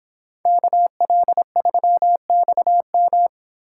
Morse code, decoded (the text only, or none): KL3XM